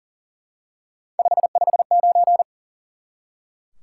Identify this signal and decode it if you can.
Morse code, heard as a keyed tone, about 39 wpm, 700 Hz. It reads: 559
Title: Top 180 QSO Elements 40wpm